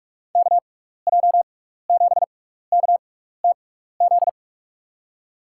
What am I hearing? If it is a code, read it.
Morse code, heard as a keyed tone, about 46 wpm, spaced as for 15 wpm, 700 Hz. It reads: KJ7KTZ
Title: USA Call Signs - 6x Character Spacing - 45wpm